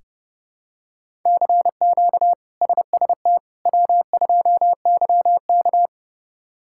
Morse code, heard as a keyed tone, about 30 wpm, 700 Hz. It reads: CQ SST W2YK
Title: K1USN SST QSO Practice - 30wpm